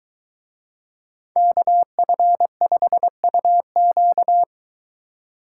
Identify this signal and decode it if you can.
Morse code, heard as a keyed tone, about 22 wpm, 700 Hz. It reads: KF5UQ